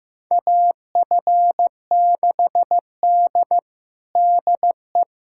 Morse code, heard as a keyed tone, about 15 wpm, 700 Hz. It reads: AF6D DE
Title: USA Callsign DE Callsign - 4 Characters - 15wpm